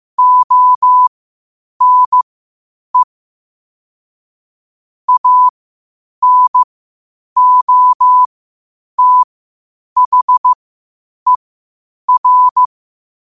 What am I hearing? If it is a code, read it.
Morse code, heard as a keyed tone, about 15 wpm, spaced as for 8 wpm, 1000 Hz. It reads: ONE ANOTHER
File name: qotdfiles / 8wpm